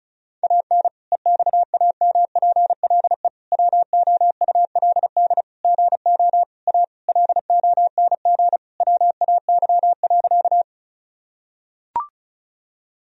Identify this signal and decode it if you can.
Morse code, heard as a keyed tone, about 36 wpm, 700 Hz. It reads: AN EXAMPLE WOULD GO A LONG WAY.
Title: Sentences from Top 1000 Words - Review Part 1 - 35wpm